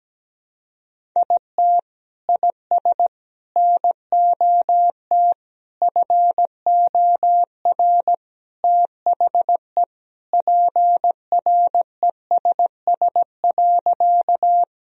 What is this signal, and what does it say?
Morse code, heard as a keyed tone, about 17 wpm, 700 Hz. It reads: IT IS NOT FOR THE PRESS.
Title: Sentences from Top 1000 Words - Review Part 1 - 17wpm